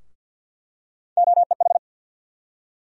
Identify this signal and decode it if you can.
Morse code, heard as a keyed tone, about 48 wpm, 700 Hz. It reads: OEH
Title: USA Call Sign Postfixes - 3 Letters 50wpm